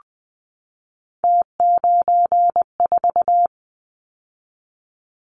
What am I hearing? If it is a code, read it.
Morse code, heard as a keyed tone, about 20 wpm, 700 Hz. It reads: T94
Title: Callsign Prefixes for Top 50 DXCC - 20wpm